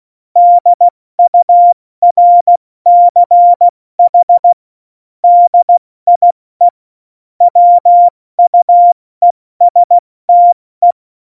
Morse code, heard as a keyed tone, about 16 wpm, 700 Hz. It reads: DURCH DIE WUESTE
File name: Morsekurs / Witze / Witz 06. November 2018, 70 F80